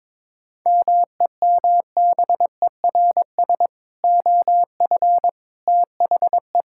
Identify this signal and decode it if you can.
Morse code, heard as a keyed tone, about 22 wpm, 700 Hz. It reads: MEMBERS OF THE
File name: practice / 22wpm / Sentences with 3-Word Phrases - 22wpm